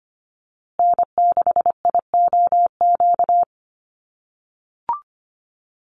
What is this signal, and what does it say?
Morse code, heard as a keyed tone, about 25 wpm, 700 Hz. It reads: N6IOQ
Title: USA Call Signs Repeated 3x 25wpm